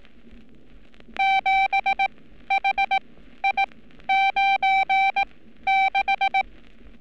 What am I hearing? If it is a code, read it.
Morse code, heard as a keyed tone, about 18 wpm, spaced as for 12 wpm, 780 Hz. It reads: 7HI96
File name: training / NAVPERS 11224-11235 / NAVPERS 11227 RA RB